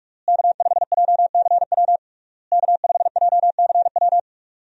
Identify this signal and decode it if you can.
Morse code, heard as a keyed tone, about 46 wpm, 700 Hz. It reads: K5JCW K5JCW
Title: USA Call Signs - 5 Characters - Repeated 3x - 45wpm